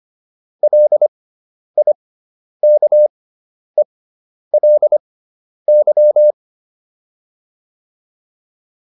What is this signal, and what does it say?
Morse code, heard as a keyed tone, about 25 wpm, spaced as for 10 wpm, 600 Hz. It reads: LIKELY